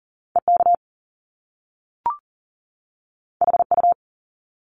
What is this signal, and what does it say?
Morse code, heard as a keyed tone, about 38 wpm, 700 Hz. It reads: EK HU
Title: Top Two Letter Combinations in Words 40wpm